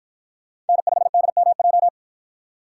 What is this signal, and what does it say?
Morse code, heard as a keyed tone, about 53 wpm, 700 Hz. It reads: N5DMJ